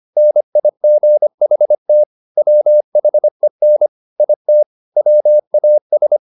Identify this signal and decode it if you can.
Morse code, heard as a keyed tone, about 25 wpm, 600 Hz. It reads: NIGHT WHEN IT WAS
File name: the old man and the sea part 1 25wpm